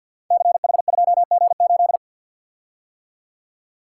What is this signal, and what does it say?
Morse code, heard as a keyed tone, about 47 wpm, 700 Hz. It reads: CH2G8
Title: Code Groups - 5 Letter:Numbers - 50wpm